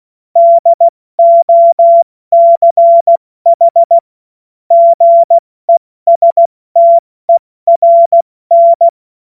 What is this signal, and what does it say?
Morse code, heard as a keyed tone, about 16 wpm, spaced as for 14 wpm, 680 Hz. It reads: DOCH GESTERN
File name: Morsekurs / Witze / Witz 30. Dezember 2025 70 F80